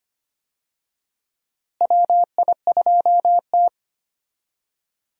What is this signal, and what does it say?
Morse code, heard as a keyed tone, about 25 wpm, 700 Hz. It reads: WI2T
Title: USA Call Signs - 4 Characters - Repeated 3x - 25wpm